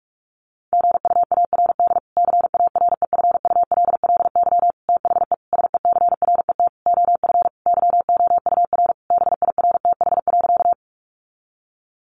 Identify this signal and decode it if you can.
Morse code, heard as a keyed tone, about 44 wpm, 700 Hz. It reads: GUARD CAREFULLY THE SECRET OF YOUR BIRTH.
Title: Sentences from Top 1300 Words 45wpm